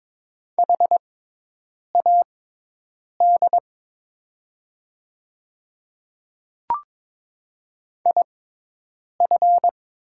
Morse code, heard as a keyed tone, about 22 wpm, spaced as for 8 wpm, 700 Hz. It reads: HAD IF
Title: 6x Character Spacing - Top 200 Words - 22wpm